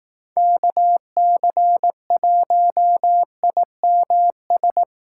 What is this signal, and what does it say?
Morse code, heard as a keyed tone, about 18 wpm, 700 Hz. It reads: KC1IMS
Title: Speed Racing - USA 2x3 Call Signs 15wpm